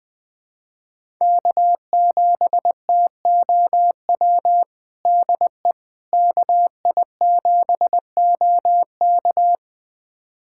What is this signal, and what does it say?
Morse code, heard as a keyed tone, about 21 wpm, 700 Hz. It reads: K7TOW DE KI7OK
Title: USA Callsign DE Callsign - 5 Characters - 20wpm